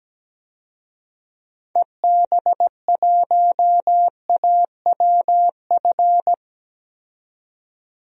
Morse code, heard as a keyed tone, about 17 wpm, 700 Hz. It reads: EB1AWF
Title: DXCC-100 Call Signs - 6 Characters - 17wpm